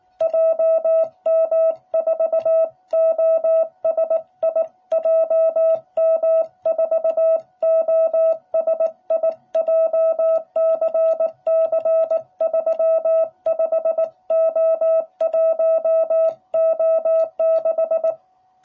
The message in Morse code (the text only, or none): JM4OSI JM4OSI JCC35O1O6